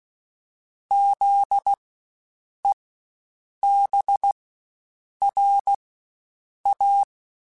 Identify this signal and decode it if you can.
Morse code, heard as a keyed tone, about 16 wpm, spaced as for 7 wpm, 800 Hz. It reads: ZEBRA
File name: Spaced Alphas / Spaced Alphas mp3s / Zs sp @15